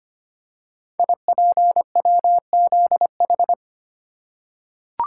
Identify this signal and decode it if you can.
Morse code, heard as a keyed tone, about 25 wpm, 700 Hz. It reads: IPWZH
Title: Code Groups - 5 Letters - 25wpm